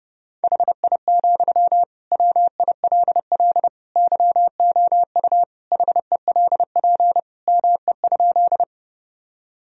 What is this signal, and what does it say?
Morse code, heard as a keyed tone, about 30 wpm, 700 Hz. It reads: HI, WILL YOU HELP ME?
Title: Sentences from Top 1000 Words - Review Part 3 - 30wpm